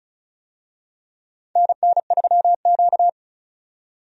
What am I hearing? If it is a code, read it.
Morse code, heard as a keyed tone, about 36 wpm, 700 Hz. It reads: NN3Q
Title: USA Call Signs - 4 Characters - 35wpm